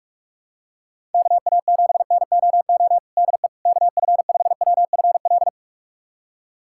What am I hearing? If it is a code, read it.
Morse code, heard as a keyed tone, about 44 wpm, 700 Hz. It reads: KA7NOO DE KF5WFL
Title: USA Callsign DE Callsign 45wpm